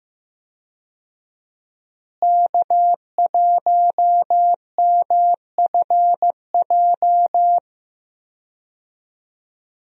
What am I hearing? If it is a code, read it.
Morse code, heard as a keyed tone, about 15 wpm, 700 Hz. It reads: K1MFJ
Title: USA Call Signs - 5 Characters - Repeated 3x - 15wpm